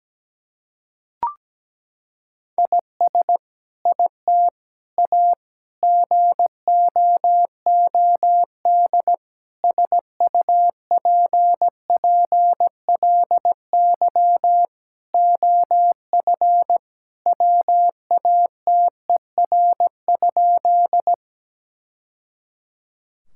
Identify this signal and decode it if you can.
Morse code, heard as a keyed tone, about 17 wpm, 700 Hz. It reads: IS IT A GOOD SUPPLY OF WATER?
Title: Sentences from Top 1000 Words - Review Part 5 - 17wpm